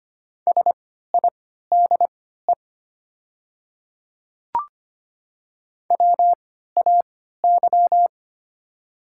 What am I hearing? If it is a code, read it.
Morse code, heard as a keyed tone, about 25 wpm, spaced as for 14 wpm, 700 Hz. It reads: SIDE WAY
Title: 3x Character Spacing - Top 200 Words - 25wpm